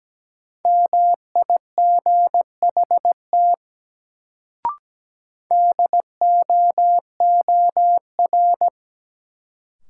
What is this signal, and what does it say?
Morse code, heard as a keyed tone, about 17 wpm, 700 Hz. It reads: MIGHT DOOR
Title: Sentences from Top 300 Words - New Word Drill - 17wpm